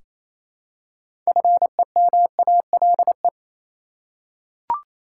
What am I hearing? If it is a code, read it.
Morse code, heard as a keyed tone, about 28 wpm, 700 Hz. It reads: FEMALE
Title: Six Letter Words - 28wpm